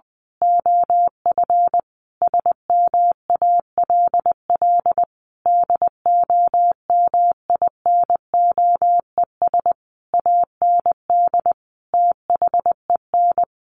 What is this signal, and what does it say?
Morse code, heard as a keyed tone, about 20 wpm, 700 Hz. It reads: OF SMALL DOMINOES AND THEN
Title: A Sound Of Thunder 20wpm